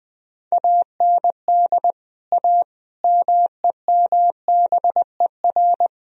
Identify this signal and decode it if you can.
Morse code, heard as a keyed tone, about 20 wpm, 700 Hz. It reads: AND A MEMBER
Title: Speed-Racing - Sets of 5 Words 17wpm